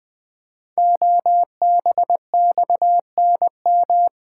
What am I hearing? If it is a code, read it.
Morse code, heard as a keyed tone, about 20 wpm, 700 Hz. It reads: OBXNM
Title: Code Groups - 5 Letters - 20wpm